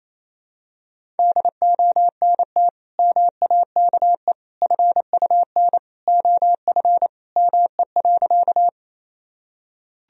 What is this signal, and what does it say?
Morse code, heard as a keyed tone, about 28 wpm, 700 Hz. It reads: DONT MAKE FUN OF ME.
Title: Sentences from Top 1000 Words - Review Part 4 - 28wpm